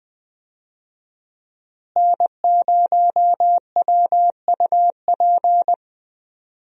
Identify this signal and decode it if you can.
Morse code, heard as a keyed tone, about 20 wpm, 700 Hz. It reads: N0WUP
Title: USA Call Signs - 5 Characters - Repeated 2x - 20wpm